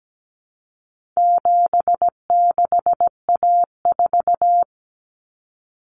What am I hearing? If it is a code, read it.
Morse code, heard as a keyed tone, about 17 wpm, 700 Hz. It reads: 76A4
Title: Common Cut Numbers - 4-Digit 17wpm